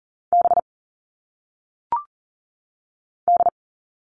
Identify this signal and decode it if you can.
Morse code, heard as a keyed tone, about 40 wpm, 700 Hz. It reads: B D
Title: All Cut Numbers - 1-Digit 40wpm